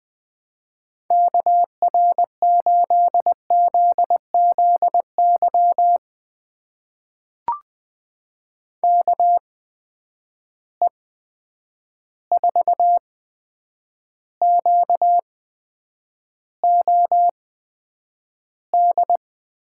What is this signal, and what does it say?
Morse code, heard as a keyed tone, about 20 wpm, spaced as for 9 wpm, 700 Hz. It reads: KR8ZZY KE4QOD
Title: USA Call Signs - 6 Characters - 8x Character Spacing - 20wpm